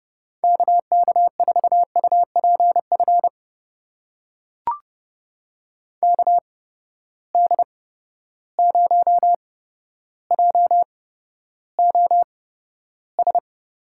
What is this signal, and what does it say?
Morse code, heard as a keyed tone, about 32 wpm, spaced as for 13 wpm, 700 Hz. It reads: KK4UPF KD0JOS
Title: USA Call Signs - 6 Characters - 8x Character Spacing - 30wpm